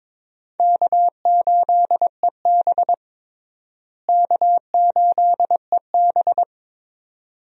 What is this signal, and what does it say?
Morse code, heard as a keyed tone, about 21 wpm, 700 Hz. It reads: K8EB K8EB
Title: USA Call Signs - 4 Characters - Repeated 2x - 22wpm